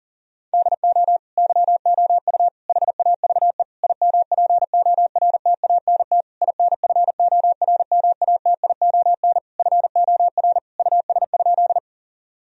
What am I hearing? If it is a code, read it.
Morse code, heard as a keyed tone, about 40 wpm, 700 Hz. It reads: DO YOU HAVE IMPORTANT INFORMATION FOR US?
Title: Sentences from Top 1000 Words - Review Part 3 - 40wpm